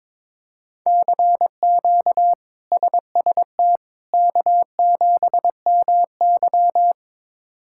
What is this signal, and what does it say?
Morse code, heard as a keyed tone, about 22 wpm, 700 Hz. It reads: CQ SST K7MY
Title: K1USN SST QSO Practice - 22wpm